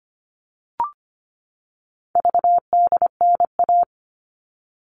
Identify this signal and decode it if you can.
Morse code, heard as a keyed tone, about 25 wpm, 700 Hz. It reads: VDNA
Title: All Cut Numbers - 4-Digit 25wpm